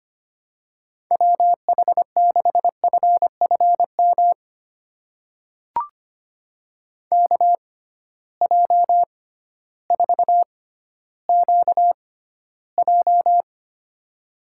Morse code, heard as a keyed tone, about 25 wpm, spaced as for 13 wpm, 700 Hz. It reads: WH6FFM KJ4QJ
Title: USA Call Signs - 6x Character Spacing - 25wpm